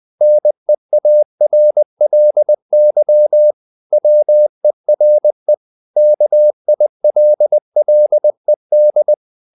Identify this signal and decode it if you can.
Morse code, heard as a keyed tone, about 20 wpm, 600 Hz. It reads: NEARLY WERE KILLED
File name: the old man and the sea / the old man and the sea part 1 20wpm